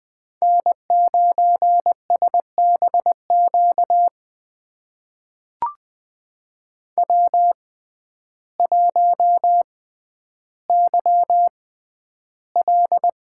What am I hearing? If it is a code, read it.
Morse code, heard as a keyed tone, about 20 wpm, spaced as for 11 wpm, 700 Hz. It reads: N9SBQ W1YL
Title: USA Call Signs - 6x Character Spacing - 20wpm